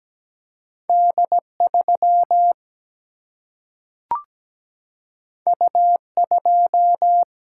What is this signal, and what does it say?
Morse code, heard as a keyed tone, about 17 wpm, 700 Hz. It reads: D 3 U 2
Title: Single-Letter Single-Number 17wpm